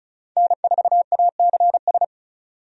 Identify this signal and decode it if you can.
Morse code, heard as a keyed tone, about 36 wpm, 700 Hz. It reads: N4ACS
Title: USA Call Signs - 5 Characters - 35wpm